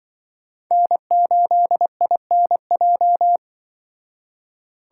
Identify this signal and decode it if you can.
Morse code, heard as a keyed tone, about 25 wpm, 700 Hz. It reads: N8INJ
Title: Speed Racing - USA Call Signs 20wpm